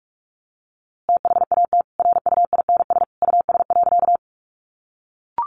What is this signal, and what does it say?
Morse code, heard as a keyed tone, about 44 wpm, 700 Hz. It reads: THAT RUINS US.
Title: A Sound Of Thunder 45wpm